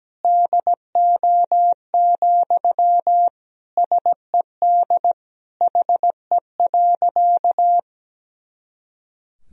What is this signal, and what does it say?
Morse code, heard as a keyed tone, about 17 wpm, 700 Hz. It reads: DO, SED HE.
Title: Sentences from Top 100 Words and Abbreviations 17wpm